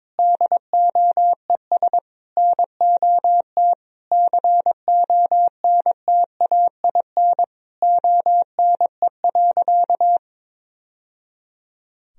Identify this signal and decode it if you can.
Morse code, heard as a keyed tone, about 22 wpm, 700 Hz. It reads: DOES NOT CONTAIN ONE.
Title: Sentences from Top 1000 Words - Review Part 2 - 22wpm